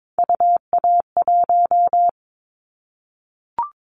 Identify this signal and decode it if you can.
Morse code, heard as a keyed tone, about 22 wpm, 700 Hz. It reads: UA1
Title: Callsign Prefixes for Top 100 DXCC - 22wpm